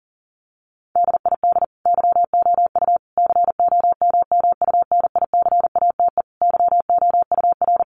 Morse code, heard as a keyed tone, about 40 wpm, 700 Hz. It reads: DID YOU COMMUNICATE YOUR
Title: Sentences from Top 1300 Words 40wpm